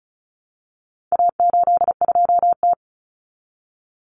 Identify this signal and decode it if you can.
Morse code, heard as a keyed tone, about 34 wpm, 700 Hz. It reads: A82T
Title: Common Cut Numbers - 4-Digit 35wpm